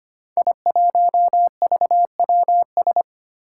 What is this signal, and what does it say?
Morse code, heard as a keyed tone, about 25 wpm, 700 Hz. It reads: I1VWS
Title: Five Challenging Letter-Number Combinations 25wpm